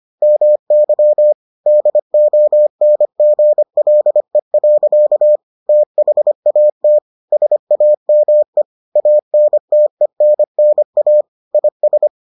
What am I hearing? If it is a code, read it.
Morse code, heard as a keyed tone, about 25 wpm, 600 Hz. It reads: MY DONGLE. THAT SAME ANTENNA IS